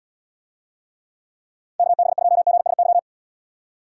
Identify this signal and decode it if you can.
Morse code, heard as a keyed tone, about 78 wpm, 700 Hz. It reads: KB2GSY